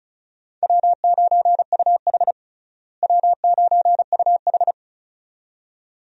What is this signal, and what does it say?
Morse code, heard as a keyed tone, about 34 wpm, 700 Hz. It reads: W9UH W9UH
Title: USA Call Signs - 4 Characters - Repeated 2x - 35wpm